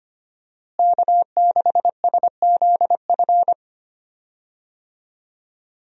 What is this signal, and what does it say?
Morse code, heard as a keyed tone, about 25 wpm, 700 Hz. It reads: K6SZF
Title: USA Call Signs - 5 Characters - Repeated 3x - 25wpm